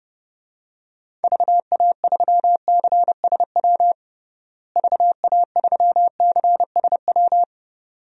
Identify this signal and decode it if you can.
Morse code, heard as a keyed tone, about 32 wpm, 700 Hz. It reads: VA3CSW VA3CSW
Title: DXCC-100 Call Signs - 6 Characters - Repeated 2x - 30wpm